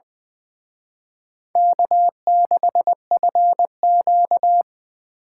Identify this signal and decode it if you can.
Morse code, heard as a keyed tone, about 25 wpm, 700 Hz. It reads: K6FQ
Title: Speed Racing - USA 1x2 Call Signs 20wpm